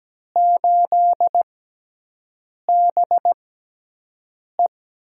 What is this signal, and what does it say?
Morse code, heard as a keyed tone, about 17 wpm, spaced as for 6 wpm, 700 Hz. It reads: 8BE
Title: USA Call Signs - 6 Characters - 6x Character Spacing - 17wpm